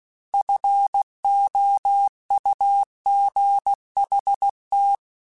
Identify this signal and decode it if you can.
Morse code, heard as a keyed tone, about 16 wpm, 800 Hz. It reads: FOUGHT
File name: F Runs / F Runs mp3s / F Runs @15